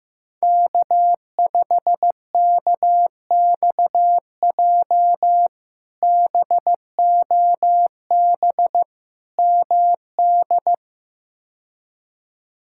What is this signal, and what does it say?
Morse code, heard as a keyed tone, about 15 wpm, 700 Hz. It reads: K5KXJ BOB MD
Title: K1USN SST QSO Practice - 15wpm